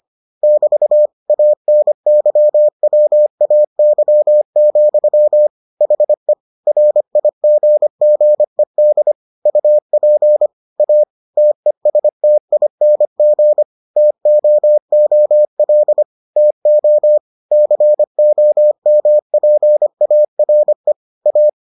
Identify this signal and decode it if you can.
Morse code, heard as a keyed tone, about 25 wpm, 600 Hz. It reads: <BT> ANYWAY, HE RIGGED UP A TESTING TOOL TO COMPARE A